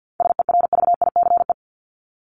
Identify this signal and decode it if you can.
Morse code, heard as a keyed tone, about 48 wpm, 700 Hz. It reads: SERVICE
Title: Top 501-750 Words 50wpm